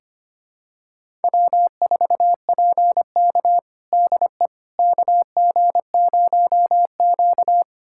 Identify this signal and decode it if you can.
Morse code, heard as a keyed tone, about 25 wpm, 700 Hz. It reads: W4PK DE KG0Q
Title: USA Callsign DE Callsign - 4 Characters - 25wpm